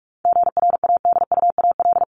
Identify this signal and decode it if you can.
Morse code, heard as a keyed tone, about 44 wpm, 700 Hz. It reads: GRADUAL
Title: LY Suffix - Top 100 Words with Base Words 45wpm